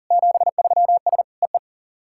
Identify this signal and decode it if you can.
Morse code, heard as a keyed tone, about 40 wpm, 690 Hz. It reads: 73S EE
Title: SOTA QSO Practice 40wpm